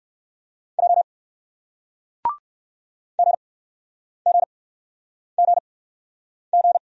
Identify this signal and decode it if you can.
Morse code, heard as a keyed tone, about 60 wpm, spaced as for 9 wpm, 700 Hz. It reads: J GGGG